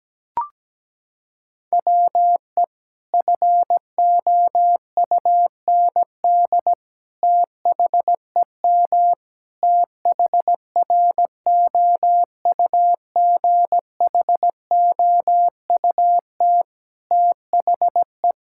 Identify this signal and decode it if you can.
Morse code, heard as a keyed tone, about 17 wpm, 700 Hz. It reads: WE FOUND THEM THROUGHOUT THE
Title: Sentences from Top 1000 Words - Review Part 5 - 17wpm